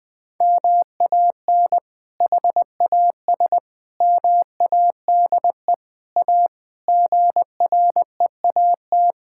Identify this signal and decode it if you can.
Morse code, heard as a keyed tone, about 20 wpm, 700 Hz. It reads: MAN HAS MADE A GREAT
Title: Sentences from Top 1000 Words - Review Part 6 - 20wpm